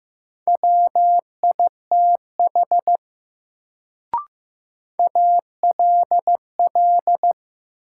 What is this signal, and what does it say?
Morse code, heard as a keyed tone, about 15 wpm, 700 Hz. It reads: WITH ALL
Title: Sentences from Top 100 Words - New Word Drill - 15wpm